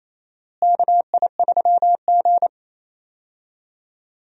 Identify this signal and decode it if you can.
Morse code, heard as a keyed tone, about 28 wpm, 700 Hz. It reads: KI3G
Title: USA Call Signs - 4 Characters - Repeated 3x - 28wpm